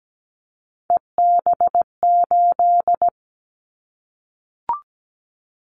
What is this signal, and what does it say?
Morse code, heard as a keyed tone, about 17 wpm, 700 Hz. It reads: EB8
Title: Callsign Prefixes for Top 50 DXCC - 17wpm